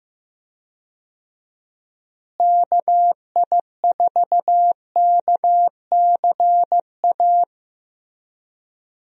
Morse code, heard as a keyed tone, about 15 wpm, 700 Hz. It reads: KI4KCA
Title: USA Call Signs - 6 Characters - Repeated 3x - 15wpm